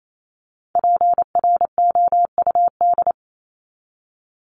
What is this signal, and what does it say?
Morse code, heard as a keyed tone, about 28 wpm, 700 Hz. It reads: PROUD
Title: Top 1501-1750 Words 28wpm